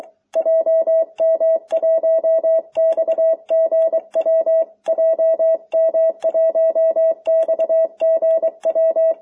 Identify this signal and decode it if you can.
Morse code, heard as a keyed tone, about 23 wpm, 650 Hz. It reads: JM1XGW JM1XGW